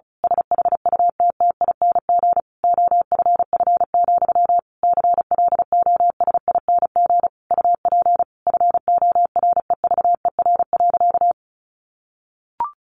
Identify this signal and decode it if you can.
Morse code, heard as a keyed tone, about 34 wpm, 700 Hz. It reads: SHUTTING OFF, CLOSING UP FOREVER.
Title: A Sound Of Thunder 35wpm